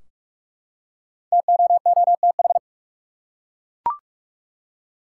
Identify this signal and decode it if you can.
Morse code, heard as a keyed tone, about 48 wpm, 700 Hz. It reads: TOOTH
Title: Five Letter Words - 45wpm